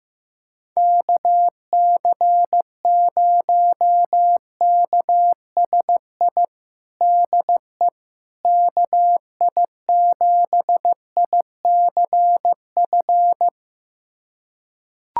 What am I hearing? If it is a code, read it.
Morse code, heard as a keyed tone, about 15 wpm, 700 Hz. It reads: KC0KSI DE KI7ICF
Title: USA Callsign DE Callsign - 6 Characters - 15wpm